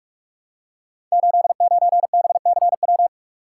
Morse code, heard as a keyed tone, about 44 wpm, 700 Hz. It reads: 89BCW
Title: Code Groups - 5 Letter:Numbers - 45wpm